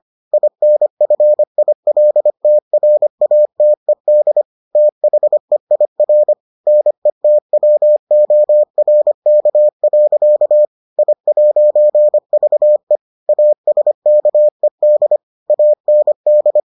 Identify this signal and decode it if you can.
Morse code, heard as a keyed tone, about 25 wpm, 600 Hz. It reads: INFILTRATED THEIR NETWORK. I'VE ASKED AND